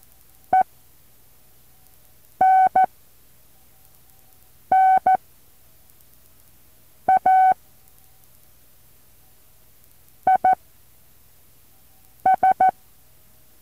Morse code, heard as a keyed tone, about 14 wpm, spaced as for 4 wpm, 760 Hz. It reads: ENNA IS
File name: intl morse code2